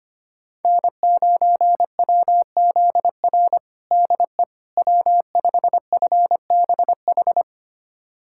Morse code, heard as a keyed tone, about 25 wpm, 700 Hz. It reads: N9WZR DE W5FBH
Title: USA Callsign DE Callsign - 5 Characters - 25wpm